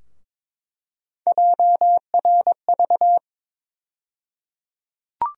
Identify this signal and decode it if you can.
Morse code, heard as a keyed tone, about 22 wpm, 700 Hz. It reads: JRV
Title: USA Call Sign Postfixes - 3 Letters 22wpm